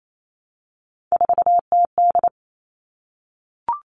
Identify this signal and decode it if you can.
Morse code, heard as a keyed tone, about 28 wpm, 700 Hz. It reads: 4TD
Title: All Cut Numbers - 3-Digit 28wpm